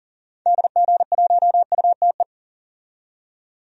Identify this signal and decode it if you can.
Morse code, heard as a keyed tone, about 37 wpm, 700 Hz. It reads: DG1UTE
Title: DXCC-100 Call Signs - 6 Characters - 40wpm